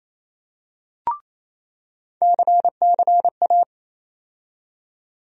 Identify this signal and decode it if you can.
Morse code, heard as a keyed tone, about 28 wpm, 700 Hz. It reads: CCA
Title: USA Call Sign Postfixes - 3 Letters 28wpm